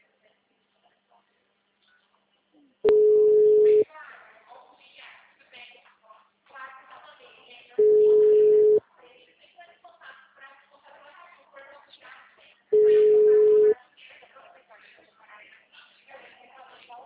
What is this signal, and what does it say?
Ringback tone (Europe, 425 Hz), with speech.